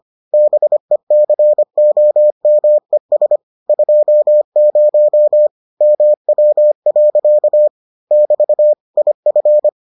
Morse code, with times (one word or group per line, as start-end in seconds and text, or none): 0.3-3.4 BECOMES
3.7-5.5 20
5.8-7.7 MW.
8.1-8.7 <BT>
9.0-9.7 IF